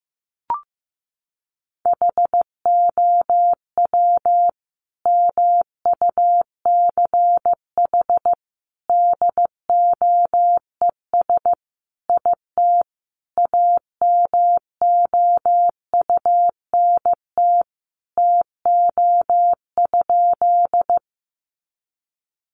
Morse code, heard as a keyed tone, about 15 wpm, 700 Hz. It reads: HOW MUCH DOES IT AMOUNT TO?
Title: Sentences from Top 700 Words 15wpm